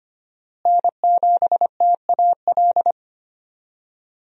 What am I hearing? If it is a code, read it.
Morse code, heard as a keyed tone, about 25 wpm, 700 Hz. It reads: N7TAL
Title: USA Call Signs - 5 Characters - Repeated 2x - 25wpm